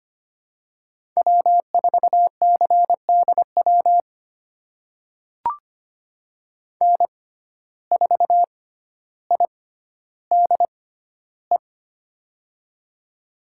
Morse code, heard as a keyed tone, about 25 wpm, spaced as for 13 wpm, 700 Hz. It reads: W4CDW N4IDE
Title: USA Call Signs - 5 Characters - 6x Character Spacing - 25wpm